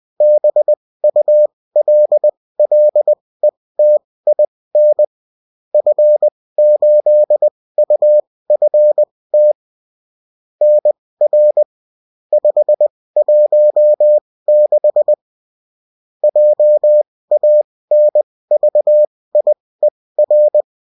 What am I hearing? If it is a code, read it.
Morse code, heard as a keyed tone, about 20 wpm, spaced as for 16 wpm, 600 Hz. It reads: BULLETIN F8UFT NR 516 JANVIER